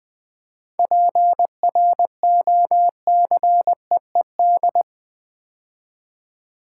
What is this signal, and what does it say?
Morse code, heard as a keyed tone, about 20 wpm, 700 Hz. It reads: PROCEED
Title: 7 Letter Words - First 3 Letters Given - 20wpm